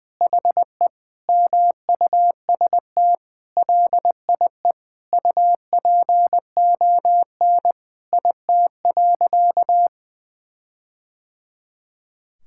Morse code, heard as a keyed tone, about 21 wpm, 700 Hz. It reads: HE MUST LIE UPON IT.
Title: Sentences from Top 1600 Words 20wpm